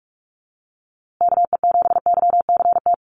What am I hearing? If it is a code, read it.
Morse code, heard as a keyed tone, about 44 wpm, 700 Hz. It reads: KE7YCT
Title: USA Call Signs Repeated 2x 45wpm